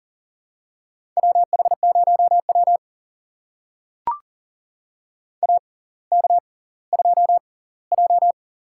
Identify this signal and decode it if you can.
Morse code, heard as a keyed tone, about 40 wpm, spaced as for 20 wpm, 700 Hz. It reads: WH0W AK2J
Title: USA Call Signs - 4 Characters - 6x Character Spacing - 40wpm